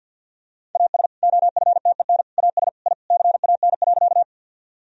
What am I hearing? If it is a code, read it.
Morse code, heard as a keyed tone, about 49 wpm, 700 Hz. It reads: AS OFTEN AS I CAN.
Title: Sentences from Top 1000 Words - Review Part 1 - 50wpm